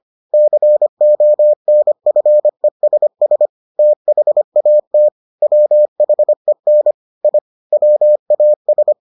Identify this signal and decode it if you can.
Morse code, heard as a keyed tone, about 25 wpm, 600 Hz. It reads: CONFESS THAT WHEN I WAS